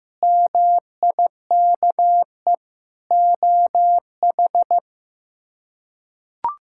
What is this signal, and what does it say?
Morse code, heard as a keyed tone, about 15 wpm, 700 Hz. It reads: MIKE OH
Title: K1USN SST QSO Practice - 15wpm